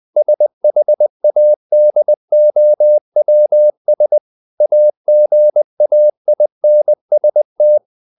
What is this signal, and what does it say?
Morse code, heard as a keyed tone, about 20 wpm, 600 Hz. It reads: SHADOWS AGAINST